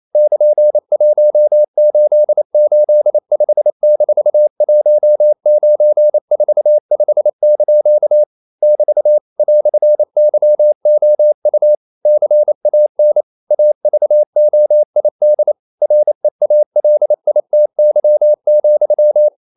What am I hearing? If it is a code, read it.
Morse code, heard as a keyed tone, about 28 wpm, 600 Hz. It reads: <KN>1885-1945) <BT> "YOU CAN AVOID REALITY,
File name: CW / Practice / EnglishQuotations0128